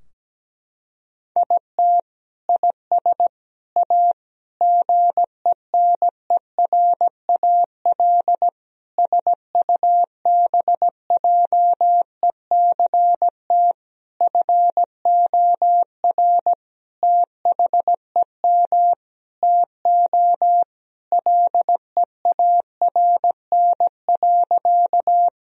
Morse code, heard as a keyed tone, about 17 wpm, 700 Hz. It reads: IT IS A GENERAL SUBJECT FOR THEM TO LEARN.
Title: Sentences from Top 1000 Words - Review Part 3 - 17wpm